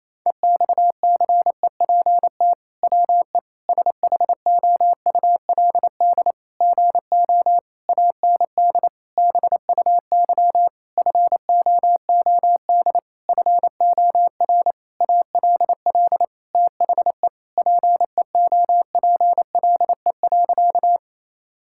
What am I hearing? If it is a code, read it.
Morse code, heard as a keyed tone, about 28 wpm, 700 Hz. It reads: EXCEPT WE SHOULD GO AND BUY FOOD FOR ALL THE PEOPLE.
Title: Sentences from Top 1600 Words 28wpm